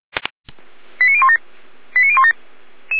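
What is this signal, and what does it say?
Five-tone selective call (ZVEI), heard as quick runs of five tones, tones of about 80 ms. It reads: 89017 89017
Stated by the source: ZVEI selective call received on SDR.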